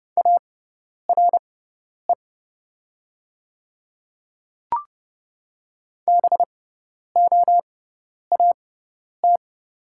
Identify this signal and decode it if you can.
Morse code, heard as a keyed tone, about 30 wpm, spaced as for 10 wpm, 700 Hz. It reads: ARE BOAT
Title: 6x Character Spacing - Top 500 Words - 30wpm